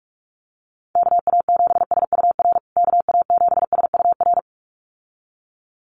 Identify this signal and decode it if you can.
Morse code, heard as a keyed tone, about 44 wpm, 700 Hz. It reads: KA7SUR KA7SUR
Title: USA Call Signs Repeated 2x 45wpm